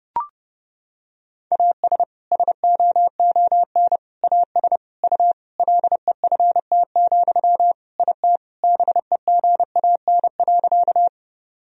Morse code, heard as a keyed tone, about 30 wpm, 700 Hz. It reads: AS SOON AS U LEFT, IT BEGAN.
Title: Sentences from Top 400 Words and Abbreviations 30wpm